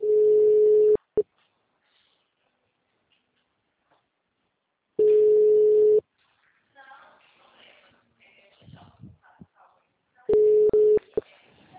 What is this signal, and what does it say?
Ringback tone (Europe, 425 Hz).